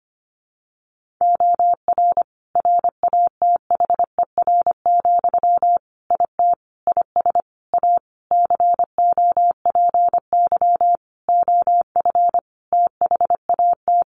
Morse code, heard as a keyed tone, about 25 wpm, 700 Hz. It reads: OR RATHER, IT IS A COPY OF THAT
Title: Sentences from Top 1400 Words 25wpm